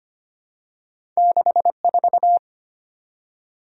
Morse code, heard as a keyed tone, about 25 wpm, 700 Hz. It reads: 64
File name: Two Letter-Number Combinations - 25wpm